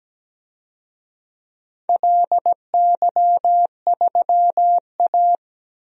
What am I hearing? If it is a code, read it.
Morse code, heard as a keyed tone, about 17 wpm, 700 Hz. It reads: LY3A
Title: DXCC-100 Call Signs - 4 Characters - 17wpm